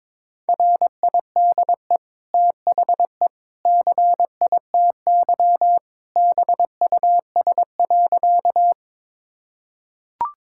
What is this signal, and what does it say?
Morse code, heard as a keyed tone, about 22 wpm, 700 Hz. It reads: RIDE THE CITY BUS.
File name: practice / 22wpm / Easy Sentences with Hints - 22wpm